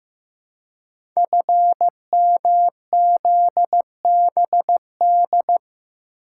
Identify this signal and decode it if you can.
Morse code, heard as a keyed tone, about 15 wpm, 700 Hz. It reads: FMZBD
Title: Code Groups - 5 Letters - 15wpm